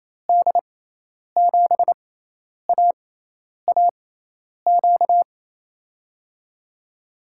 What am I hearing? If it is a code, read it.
Morse code, heard as a keyed tone, about 28 wpm, spaced as for 10 wpm, 700 Hz. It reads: D7AAQ
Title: USA Call Signs - 6 Characters - 6x Character Spacing - 28wpm